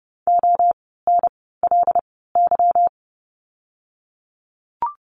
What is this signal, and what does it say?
Morse code, heard as a keyed tone, about 30 wpm, spaced as for 17 wpm, 700 Hz. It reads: ONLY
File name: course / 30:10wpm / 230 Lesson 37f - Letter x - Words 30wpm